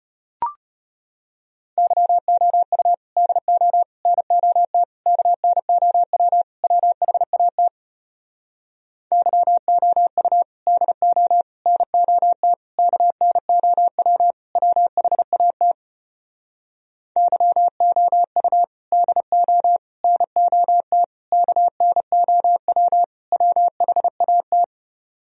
Morse code, heard as a keyed tone, about 33 wpm, 700 Hz. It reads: YOU DO NOT KNOW WHAT YOU DO NOT KNOW WHAT YOU DO NOT KNOW WHAT